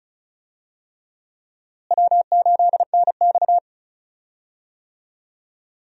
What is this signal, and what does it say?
Morse code, heard as a keyed tone, about 36 wpm, 700 Hz. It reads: W8NX
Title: USA Call Signs - 4 Characters - Repeated 3x - 35wpm